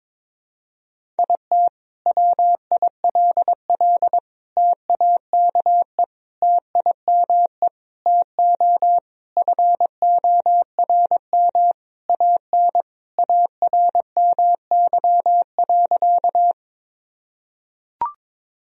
Morse code, heard as a keyed tone, about 22 wpm, 700 Hz. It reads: IT WILL TAKE TIME TO FORM AN ARMY.
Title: Sentences from Top 1000 Words - Review Part 1 - 22wpm